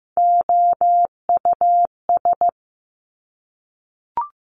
Morse code, heard as a keyed tone, about 15 wpm, 700 Hz. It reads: OUS
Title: Top Three Letter Combinations in Words 15wpm